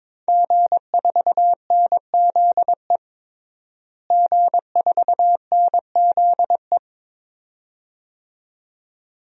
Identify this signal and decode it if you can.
Morse code, heard as a keyed tone, about 22 wpm, 700 Hz. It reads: G4NZE G4NZE
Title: DXCC-100 Call Signs - 5 Characters - Repeated 2x - 22wpm